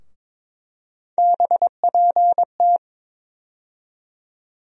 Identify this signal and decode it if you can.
Morse code, heard as a keyed tone, about 23 wpm, 700 Hz. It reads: BPT
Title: USA Call Sign Postfixes - 3 Letters 22wpm